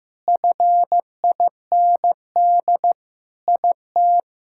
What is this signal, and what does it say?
Morse code, heard as a keyed tone, about 15 wpm, 700 Hz. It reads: FIND IT
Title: Sentences from Top 1000 Words - Review Part 6 - 15wpm